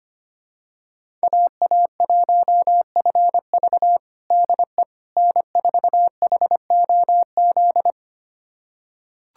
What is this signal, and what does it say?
Morse code, heard as a keyed tone, about 25 wpm, 700 Hz. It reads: AA1FV DE N4HOZ
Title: USA Callsign DE Callsign - 5 Characters - 25wpm